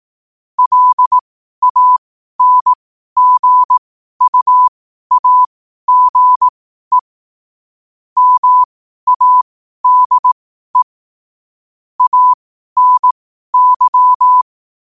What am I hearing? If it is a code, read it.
Morse code, heard as a keyed tone, about 18 wpm, spaced as for 12 wpm, 1000 Hz. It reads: LANGUAGE MADE ANY
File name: qotdfiles / 12wpm